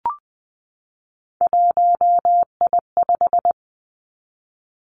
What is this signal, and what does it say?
Morse code, heard as a keyed tone, about 20 wpm, 700 Hz. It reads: 1I5
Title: Three Challenging Letter-Number Combinations 20wpm